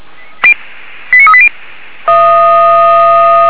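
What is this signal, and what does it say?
Five-tone selective call (ZVEI), heard as quick runs of five tones, tones of about 70 ms. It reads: 89380
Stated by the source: ZVEI five-tone selective call received on an SDR.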